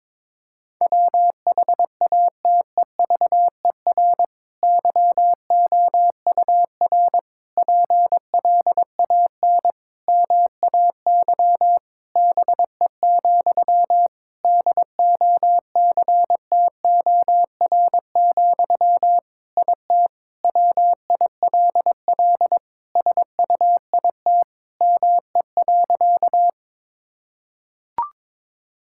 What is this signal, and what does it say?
Morse code, heard as a keyed tone, about 22 wpm, 700 Hz. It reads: WHATEVER YOUR PLAN MAY BE, DOCTOR, IT WILL SUIT ME.
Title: Sentences from Top 1600 Words 22wpm